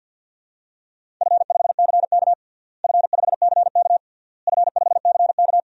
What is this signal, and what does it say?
Morse code, heard as a keyed tone, about 51 wpm, 700 Hz. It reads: F5CK F5CK F5CK
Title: DXCC-100 Call Signs - 4 Characters - Repeated 3x - 50wpm